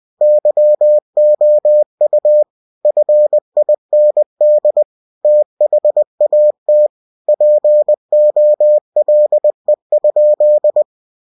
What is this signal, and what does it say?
Morse code, heard as a keyed tone, about 20 wpm, 600 Hz. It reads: YOU FIND THAT POLE?
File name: cw-books / winnie-the-pooh / winnie-20wpm-page-0087